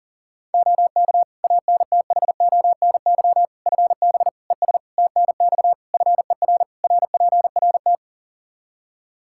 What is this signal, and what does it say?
Morse code, heard as a keyed tone, about 40 wpm, 700 Hz. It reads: OK ANTHONY FB ES TNX FER RPRT
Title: Practice QSO - Exchange 2 - Info - 40wpm